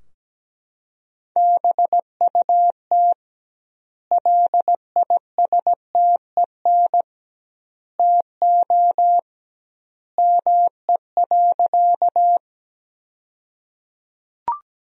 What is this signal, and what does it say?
Morse code, heard as a keyed tone, about 17 wpm, 700 Hz. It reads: BUT LISTEN TO ME.
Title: 2x Word-Spacing - Sentences from Top 400 Words - 17wpm